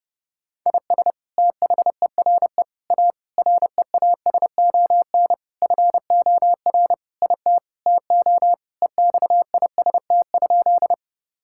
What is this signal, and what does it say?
Morse code, heard as a keyed tone, about 30 wpm, 700 Hz. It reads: IS THERE A REASON FOR IT TO EXIST?
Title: Sentences from Top 1000 Words - Review Part 5 - 30wpm